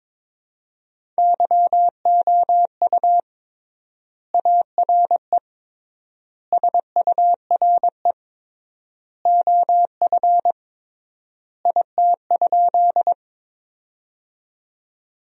Morse code, heard as a keyed tone, about 21 wpm, 700 Hz. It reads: YOU ARE SURE OF IT?
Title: 3x Word-Spacing - Sentences from Top 300 Words - 22wpm